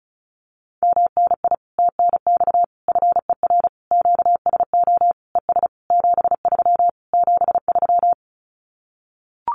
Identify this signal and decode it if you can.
Morse code, heard as a keyed tone, about 34 wpm, 700 Hz. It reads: MNI TNX FER QSO ES 73 73
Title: Common QSO Phrases Encore 35wpm